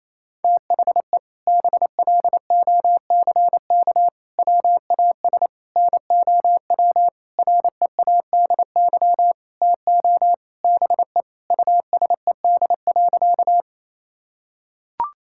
Morse code, heard as a keyed tone, about 28 wpm, 700 Hz. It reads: THE BLOCK WAS NOW READY TO BE USED.
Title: Sentences from Top 1600 Words 28wpm